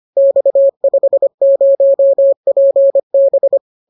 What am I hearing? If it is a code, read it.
Morse code, heard as a keyed tone, about 25 wpm, 550 Hz. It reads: X50PB